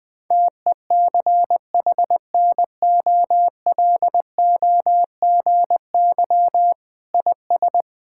Morse code, heard as a keyed tone, about 20 wpm, 700 Hz. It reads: TECHNOLOGY IS
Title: Sentences from Top 1000 Words - Review Part 5 - 20wpm